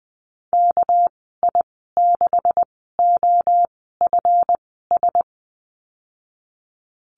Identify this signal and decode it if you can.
Morse code, heard as a keyed tone, about 20 wpm, spaced as for 15 wpm, 700 Hz. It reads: KI6OFS
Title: 245 Lesson 39g - Letter z - Call Signs 20wpm